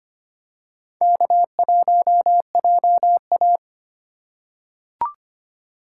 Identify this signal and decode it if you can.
Morse code, heard as a keyed tone, about 25 wpm, 700 Hz. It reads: K1JA
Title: USA Call Signs - 4 Characters - 4x Character Spacing - 25wpm